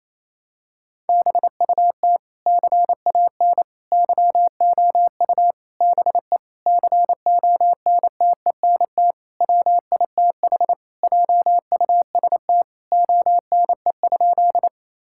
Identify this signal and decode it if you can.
Morse code, heard as a keyed tone, about 27 wpm, 700 Hz. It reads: BUT CAN YOU BE CONTENT WITH JUST ONE?
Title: Sentences from Top 1000 Words - Review Part 6 - 28wpm